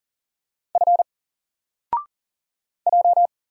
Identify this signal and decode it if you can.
Morse code, heard as a keyed tone, about 40 wpm, 700 Hz. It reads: F J
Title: Single Letter-Number Alt 40wpm